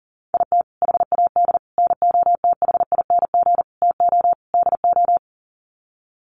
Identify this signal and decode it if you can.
Morse code, heard as a keyed tone, about 40 wpm, 700 Hz. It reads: IT HAD NOTHING TO DO
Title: Sets of 5 Words Encore2 40wpm